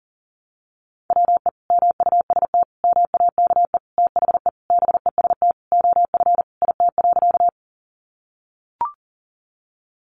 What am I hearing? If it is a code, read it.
Morse code, heard as a keyed tone, about 40 wpm, 700 Hz. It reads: WE MUST MAKE THE BEST OF IT.
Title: Sentences from Top 400 Words 40wpm